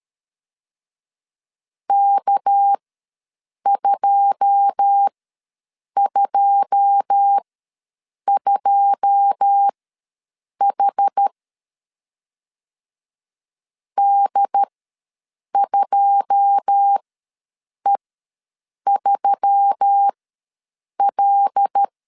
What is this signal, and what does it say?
Morse code, heard as a keyed tone, about 13 wpm, spaced as for 7 wpm, 800 Hz. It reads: K222H D2E3L
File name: LEZIONI / 9 2 ws 12wpm 3406